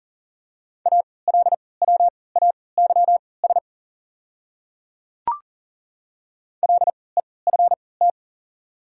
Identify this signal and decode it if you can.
Morse code, heard as a keyed tone, about 37 wpm, spaced as for 22 wpm, 700 Hz. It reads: ALWAYS LEFT
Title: 3x Character Spacing - Top 300 Words - 40wpm